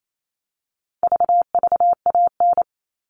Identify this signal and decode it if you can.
Morse code, heard as a keyed tone, about 28 wpm, 700 Hz. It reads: VVAN
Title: All Cut Numbers - 4-Digit 28wpm